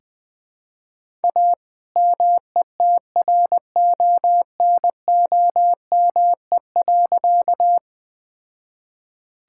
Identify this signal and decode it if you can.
Morse code, heard as a keyed tone, about 20 wpm, 700 Hz. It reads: A METRONOME.
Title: All-Age Appropriate Jokes - 20wpm